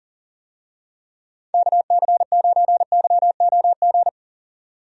Morse code, heard as a keyed tone, about 40 wpm, 700 Hz. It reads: KC9YOG
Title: USA Call Signs - 6 Characters - 40wpm